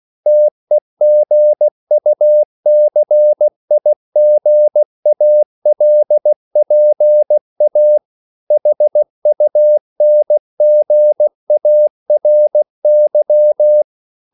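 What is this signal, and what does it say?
Morse code, heard as a keyed tone, about 16 wpm, 600 Hz. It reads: TEGUCIGALPA HUNGARY
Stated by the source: CW practice of national capitals at 16 words per minute, from ON6ZQ.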